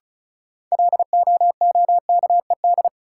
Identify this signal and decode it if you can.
Morse code, heard as a keyed tone, about 34 wpm, 700 Hz. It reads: LOOKED
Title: Sentences from Top 800 Words - New Word Drill - 35wpm